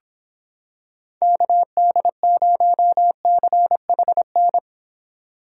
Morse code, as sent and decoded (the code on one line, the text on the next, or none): -.- -.. ----- -.-. .... -.
KD0CHN